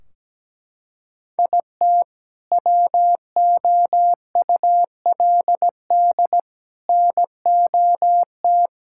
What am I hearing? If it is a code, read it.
Morse code, heard as a keyed tone, about 17 wpm, 700 Hz. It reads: IT WOULD NOT
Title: Sentences from Top 1600 Words 17wpm